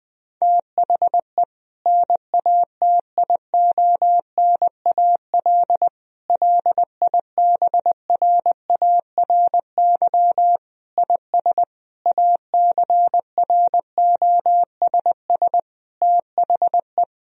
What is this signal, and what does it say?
Morse code, heard as a keyed tone, about 20 wpm, 700 Hz. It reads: THE NATIONAL LIBRARY IS ACROSS THE
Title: Sentences from Top 1000 Words - Review Part 4 - 20wpm